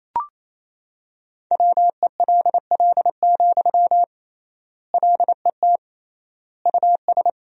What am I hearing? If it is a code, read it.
Morse code, heard as a keyed tone, about 27 wpm, 700 Hz. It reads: WELL, LET US
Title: 3x Word-Spacing - Sentences from Top 300 Words - 28wpm